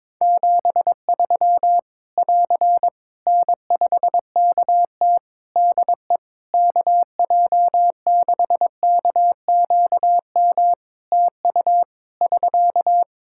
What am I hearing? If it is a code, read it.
Morse code, heard as a keyed tone, about 22 wpm, 700 Hz. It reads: 73 <AR> N5KT DE KJ6KQM TU <SK>
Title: Practice QSO - Exchange 4 - Ending - 22wpm